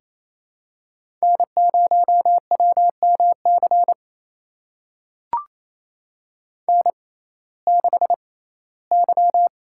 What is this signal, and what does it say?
Morse code, heard as a keyed tone, about 28 wpm, spaced as for 17 wpm, 700 Hz. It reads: N0WMC N6Y
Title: USA Call Signs - 5 Characters - 6x Character Spacing - 28wpm